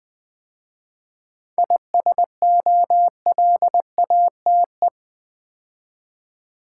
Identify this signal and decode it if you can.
Morse code, heard as a keyed tone, about 20 wpm, 700 Hz. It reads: ISOLATE
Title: 7 Letter Words - First 3 Letters Given - 20wpm